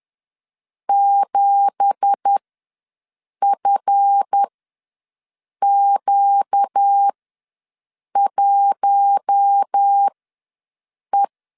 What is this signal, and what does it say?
Morse code, heard as a keyed tone, about 10 wpm, spaced as for 6 wpm, 800 Hz. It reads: 7FQ1E
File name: LEZIONI / 9 2867 ws 10wpm 4215